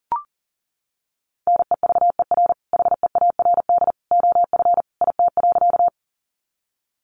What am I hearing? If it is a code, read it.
Morse code, heard as a keyed tone, about 40 wpm, 700 Hz. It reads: NEVER HEARD OF IT.
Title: Sentences from Top 400 Words 40wpm